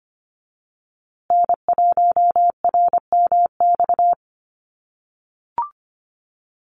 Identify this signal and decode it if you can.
Morse code, heard as a keyed tone, about 25 wpm, 700 Hz. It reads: N1RMX
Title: USA Call Signs Repeated 2x 25wpm